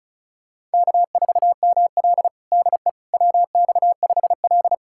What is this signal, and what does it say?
Morse code, heard as a keyed tone, about 36 wpm, 700 Hz. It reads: K4ML DE WX5L
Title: USA Callsign DE Callsign - 4 Characters - 35wpm